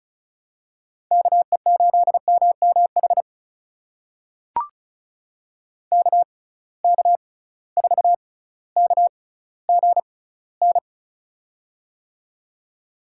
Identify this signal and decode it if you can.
Morse code, heard as a keyed tone, about 34 wpm, spaced as for 18 wpm, 700 Hz. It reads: KE8MMH KK4KGN
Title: USA Call Signs - 6 Characters - 6x Character Spacing - 35wpm